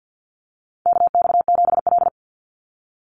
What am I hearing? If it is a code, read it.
Morse code, heard as a keyed tone, about 49 wpm, 700 Hz. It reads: KX7L
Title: NA SOTA Chaser Callsigns 50wpm